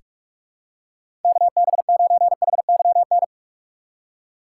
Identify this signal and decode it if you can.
Morse code, heard as a keyed tone, about 41 wpm, 700 Hz. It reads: KB9HYN
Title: USA Call Signs - 6 Characters - 4x Character Spacing - 45wpm